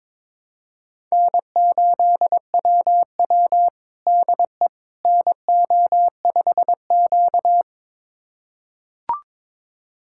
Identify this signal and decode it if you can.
Morse code, heard as a keyed tone, about 22 wpm, 700 Hz. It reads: N8WW DE NO5Q
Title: USA Callsign DE Callsign - 4 Characters - 22wpm